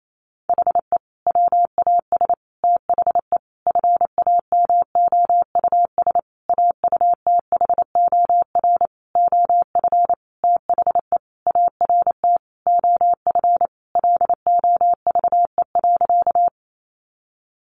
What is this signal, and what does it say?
Morse code, heard as a keyed tone, about 28 wpm, 700 Hz. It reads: HE WAS THE FAMOUS AUTHOR OF THE ART OF LOVE.
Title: Sentences from Top 1500 Words 28wpm